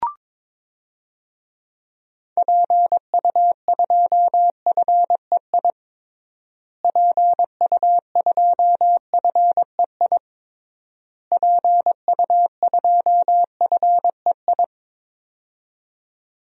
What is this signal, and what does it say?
Morse code, heard as a keyed tone, about 21 wpm, 700 Hz. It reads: PU2FEI PU2FEI PU2FEI
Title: DXCC-100 Call Signs - 6 Characters - Repeated 3x - 22wpm